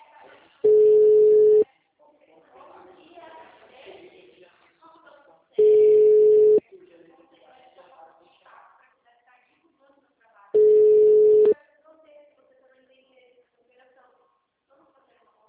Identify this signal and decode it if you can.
Ringback tone (Europe, 425 Hz), with speech.